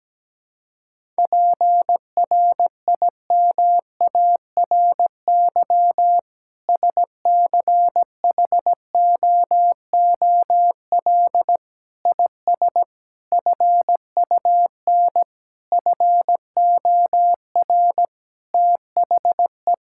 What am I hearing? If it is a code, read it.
Morse code, heard as a keyed tone, about 17 wpm, 700 Hz. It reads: PRIMARY SCHOOL IS FUN FOR THE
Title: Sentences from Top 1000 Words - Review Part 5 - 17wpm